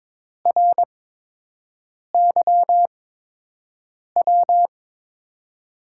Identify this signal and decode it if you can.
Morse code, heard as a keyed tone, about 22 wpm, spaced as for 6 wpm, 700 Hz. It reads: RYW
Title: USA Call Signs - 8x Character Spacing - 22wpm